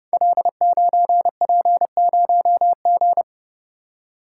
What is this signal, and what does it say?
Morse code, heard as a keyed tone, about 30 wpm, 700 Hz. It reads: L9P0G
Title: Code Groups - 5 Letter:Numbers - 30wpm